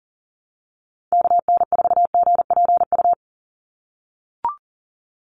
Morse code, heard as a keyed tone, about 40 wpm, 700 Hz. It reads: KN4GPU
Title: USA Call Signs 40wpm